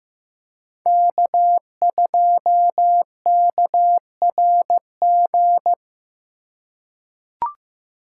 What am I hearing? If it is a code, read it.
Morse code, heard as a keyed tone, about 15 wpm, 700 Hz. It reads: K2KRG
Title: USA Call Signs - 5 Characters - 8x Character Spacing - 15wpm